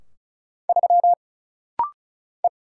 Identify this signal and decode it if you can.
Morse code, heard as a keyed tone, about 36 wpm, 700 Hz. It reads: 3 E
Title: Single Letter-Number - Rapid-Fire 35wpm